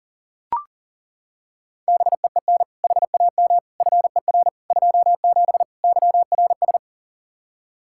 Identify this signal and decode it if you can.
Morse code, heard as a keyed tone, about 40 wpm, 700 Hz. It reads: BEEN HAM FER 27 YRS
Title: Practice QSO - Exchange 3 - More Info - 40wpm